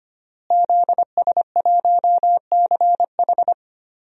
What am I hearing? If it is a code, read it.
Morse code, heard as a keyed tone, about 25 wpm, 700 Hz. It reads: ZS1CH
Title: DXCC-100 Call Signs - 5 Characters - Repeated 3x - 25wpm